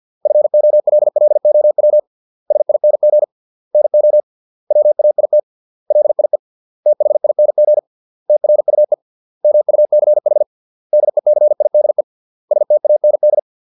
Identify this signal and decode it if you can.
Morse code, heard as a keyed tone, about 50 wpm, 600 Hz. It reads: FOLLOW SING NO WAIT LIE THING TRUE MUCH DECIDE STAND